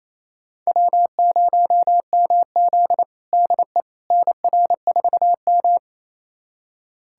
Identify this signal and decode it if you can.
Morse code, heard as a keyed tone, about 28 wpm, 700 Hz. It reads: W0MZ DE NR4M
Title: USA Callsign DE Callsign - 4 Characters - 28wpm